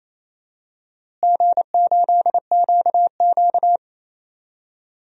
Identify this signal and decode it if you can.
Morse code, heard as a keyed tone, about 27 wpm, 700 Hz. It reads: G8QQ
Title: DXCC-100 Call Signs - 4 Characters - Repeated 2x - 28wpm